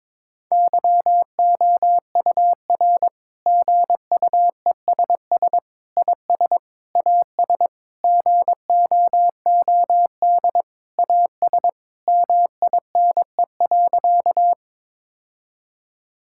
Morse code, heard as a keyed tone, about 22 wpm, 700 Hz. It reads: YOUR GUESS IS AS GOOD AS MINE.
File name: Sentences with 7-Word Phrases - 22wpm